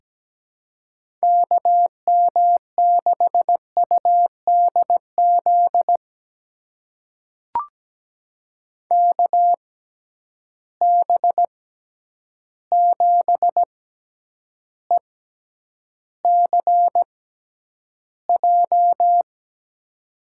Morse code, heard as a keyed tone, about 17 wpm, spaced as for 9 wpm, 700 Hz. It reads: KM6UDZ KB7ECJ
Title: USA Call Signs - 6 Characters - 6x Character Spacing - 17wpm